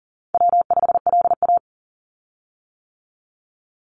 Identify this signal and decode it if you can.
Morse code, heard as a keyed tone, about 41 wpm, 700 Hz. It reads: W5LA
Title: Call Signs - 40wpm